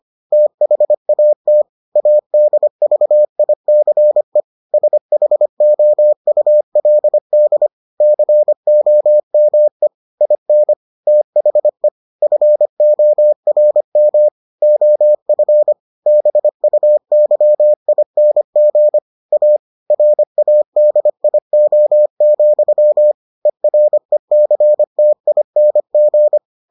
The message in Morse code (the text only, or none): THAT ADVICE SHOULD COME IN THE FORM OF BUYING A RADIO, ERECTING